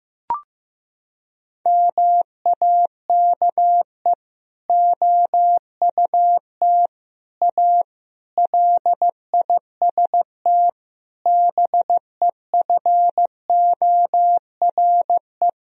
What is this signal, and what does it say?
Morse code, heard as a keyed tone, about 15 wpm, 700 Hz. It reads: MAKE OUT A LIST BEFORE
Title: Sentences from Top 1000 Words - Review Part 2 - 15wpm